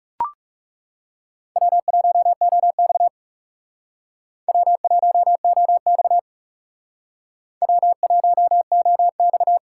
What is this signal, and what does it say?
Morse code, heard as a keyed tone, about 38 wpm, 700 Hz. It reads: W1OX W1OX W1OX
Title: Speed Racing - USA 1x2 Call Signs 30wpm